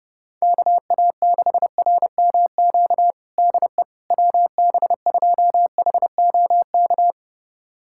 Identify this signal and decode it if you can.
Morse code, heard as a keyed tone, about 32 wpm, 700 Hz. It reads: KA6RMQ DE WB2HOK
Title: USA Callsign DE Callsign - 6 Characters - 30wpm